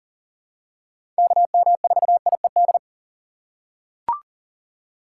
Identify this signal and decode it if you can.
Morse code, heard as a keyed tone, about 40 wpm, 700 Hz. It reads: KM4IED
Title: USA Call Signs - 2x Character Spacing - 40wpm